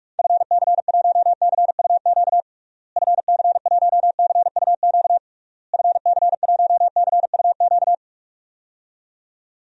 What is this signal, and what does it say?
Morse code, heard as a keyed tone, about 41 wpm, 700 Hz. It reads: FC1CUQ FC1CUQ FC1CUQ
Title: DXCC-100 Call Signs - 6 Characters - Repeated 3x - 45wpm